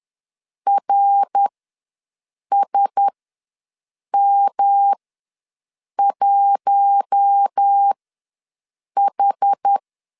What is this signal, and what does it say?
Morse code, heard as a keyed tone, about 10 wpm, spaced as for 6 wpm, 800 Hz. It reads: RSM1H
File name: LEZIONI / 5 LBVC / 5 L ws 10wpm 3636